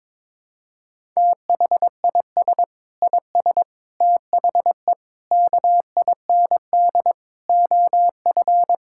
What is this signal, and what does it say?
Morse code, heard as a keyed tone, about 22 wpm, 700 Hz. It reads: THIS IS THE KIND OF
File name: Sentences with 7-Word Phrases - 22wpm